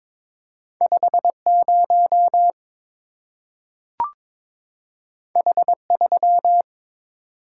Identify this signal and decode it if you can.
Morse code, heard as a keyed tone, about 22 wpm, 700 Hz. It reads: 50 H3
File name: practice / 22wpm / Two Letter-Number Combinations - 22wpm